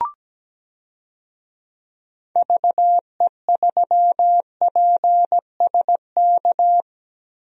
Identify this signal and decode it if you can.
Morse code, heard as a keyed tone, about 17 wpm, 700 Hz. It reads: VE3PSK
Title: DXCC-100 Call Signs - 6 Characters - Repeated 3x - 17wpm